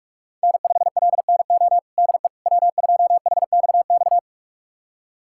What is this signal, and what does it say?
Morse code, heard as a keyed tone, about 42 wpm, 700 Hz. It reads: N5LNO DE W2HXX
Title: USA Callsign DE Callsign - 5 Characters - 45wpm